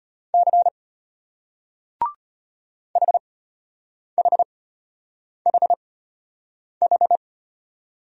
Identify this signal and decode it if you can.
Morse code, heard as a keyed tone, about 35 wpm, spaced as for 8 wpm, 700 Hz. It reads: C HHHH